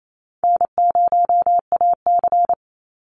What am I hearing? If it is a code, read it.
Morse code, heard as a keyed tone, about 28 wpm, 700 Hz. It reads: N0AC
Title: Easy Call Signs - 28wpm